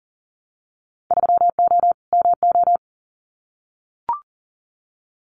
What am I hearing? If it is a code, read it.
Morse code, heard as a keyed tone, about 40 wpm, 700 Hz. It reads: 3O MO
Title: Winter Field Day Exchanges Encore 40wpm